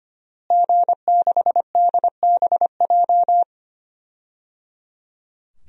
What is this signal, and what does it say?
Morse code, heard as a keyed tone, about 25 wpm, 700 Hz. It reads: G6DBJ
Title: DXCC-100 Call Signs - 5 Characters - Repeated 3x - 25wpm